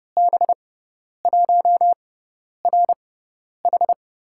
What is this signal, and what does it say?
Morse code, heard as a keyed tone, about 30 wpm, spaced as for 10 wpm, 700 Hz. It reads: B1RH
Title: USA Call Signs - 5 Characters - 6x Character Spacing - 30wpm